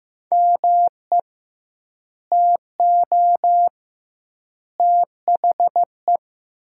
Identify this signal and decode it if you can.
Morse code, heard as a keyed tone, about 16 wpm, 700 Hz. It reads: ME TO THE
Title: 2x Word-Spacing - Sentences from Top 400 Words - 15wpm